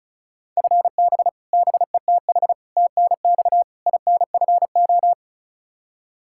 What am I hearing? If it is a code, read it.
Morse code, heard as a keyed tone, about 36 wpm, 700 Hz. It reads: FB BETH TNX INFO
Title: Practice QSO - Exchange 3 - More Info - 35wpm